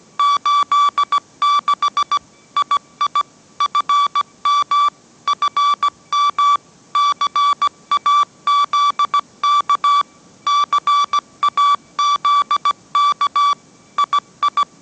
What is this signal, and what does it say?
Morse code, heard as a keyed tone, about 17 wpm, 1150 Hz. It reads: 86 II FM FM CAZK CAZK II